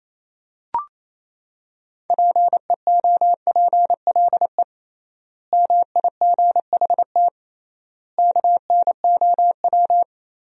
Morse code, heard as a keyed tone, about 28 wpm, 700 Hz. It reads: PEOPLE MIGHT KNOW
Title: 3x Word-Spacing - Sentences from Top 500 Words - 28wpm